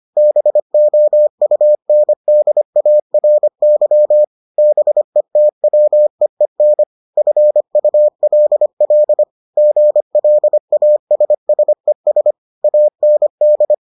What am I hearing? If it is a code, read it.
Morse code, heard as a keyed tone, about 25 wpm, 600 Hz. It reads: BOUNDARY BETWEEN FULL GLASSES AND